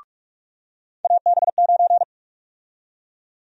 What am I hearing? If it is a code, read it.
Morse code, heard as a keyed tone, about 46 wpm, 700 Hz. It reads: AB9
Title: USA Call Sign Prefixes - 45wpm